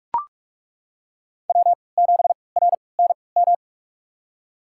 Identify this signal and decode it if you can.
Morse code, heard as a keyed tone, about 41 wpm, spaced as for 25 wpm, 700 Hz. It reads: W7RNM
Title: USA Call Signs - 5 Characters - 3x Character Spacing - 45wpm